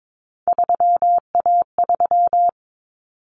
Morse code, heard as a keyed tone, about 22 wpm, 700 Hz. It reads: 3A3
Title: Common Cut Numbers - 3-Digit 22wpm